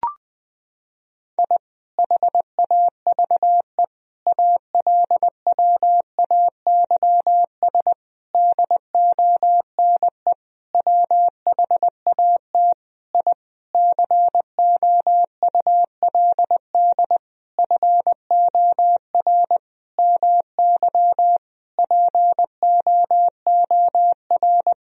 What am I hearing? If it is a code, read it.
Morse code, heard as a keyed tone, about 20 wpm, 700 Hz. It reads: I HAVE ALWAYS DONE WHAT I COULD FOR MY POOR
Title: Sentences from Top 1000 Words - Review Part 2 - 20wpm